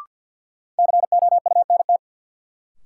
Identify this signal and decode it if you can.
Morse code, heard as a keyed tone, about 53 wpm, 700 Hz. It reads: COUNT
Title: Sentences from Top 1500 Words - New Word Drill - 50wpm